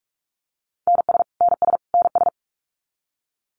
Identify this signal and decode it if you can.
Morse code, heard as a keyed tone, about 44 wpm, 700 Hz. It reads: NS NS NS
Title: Canadian Provinces and Territories - Repeated 3x - 45wpm